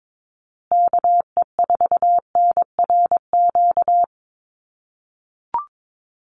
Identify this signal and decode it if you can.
Morse code, heard as a keyed tone, about 22 wpm, 700 Hz. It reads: KE4NRQ
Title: USA Call Signs Repeated 2x 22wpm